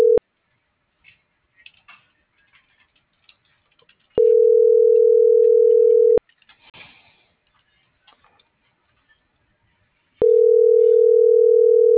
Ringback tone (North America), 440 + 480 Hz.